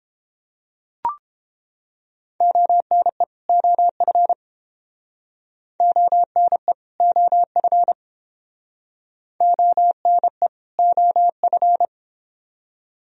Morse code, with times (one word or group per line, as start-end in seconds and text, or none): 2.4-3.2 ONE
3.5-4.3 OF
5.8-6.7 ONE
7.0-7.9 OF
9.4-10.5 ONE
10.8-11.9 OF